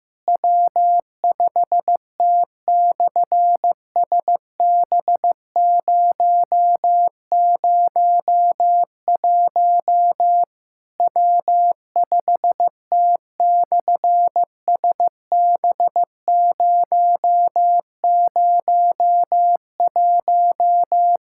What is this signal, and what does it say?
Morse code, heard as a keyed tone, about 15 wpm, 700 Hz. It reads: W5T/SB001 W5T/SB001
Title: SOTA QSO Practice 15wpm